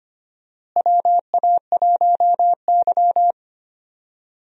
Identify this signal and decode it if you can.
Morse code, heard as a keyed tone, about 25 wpm, 700 Hz. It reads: WA1Y
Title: USA Call Signs - 4 Characters - Repeated 2x - 25wpm